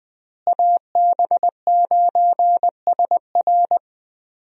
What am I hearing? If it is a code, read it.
Morse code, heard as a keyed tone, about 21 wpm, 700 Hz. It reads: AB9SR
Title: USA Call Signs - 5 Characters - Repeated 2x - 20wpm